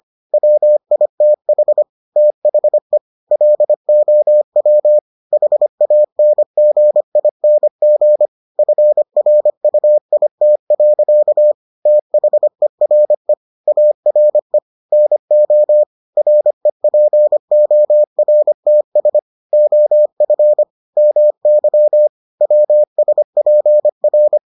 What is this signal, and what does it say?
Morse code, heard as a keyed tone, about 25 wpm, 600 Hz. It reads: WITH THE LOW HANGING FRUIT. THERE ARE NO REPORTS OF MY WSPR